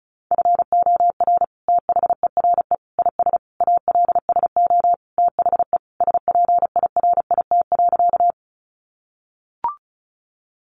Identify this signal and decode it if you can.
Morse code, heard as a keyed tone, about 34 wpm, 700 Hz. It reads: FOR THERE IS ALSO THE SPIRIT.
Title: Sentences from Top 1400 Words 35wpm